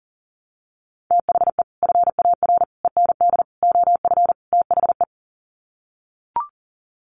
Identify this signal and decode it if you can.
Morse code, heard as a keyed tone, about 40 wpm, 700 Hz. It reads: THE FAR END OF THE
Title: Sets of 5 Words 40wpm